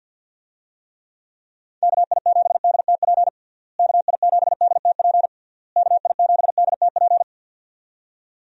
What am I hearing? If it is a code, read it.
Morse code, heard as a keyed tone, about 50 wpm, 700 Hz. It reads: KI7DTP KI7DTP KI7DTP
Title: USA Call Signs - 6 Characters - Repeated 3x - 50wpm